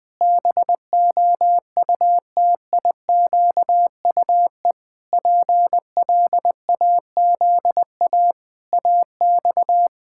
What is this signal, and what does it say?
Morse code, heard as a keyed tone, about 20 wpm, 700 Hz. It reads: BOUTIQUE PLAZA AX
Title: Pangram Challenge v2 - 20wpm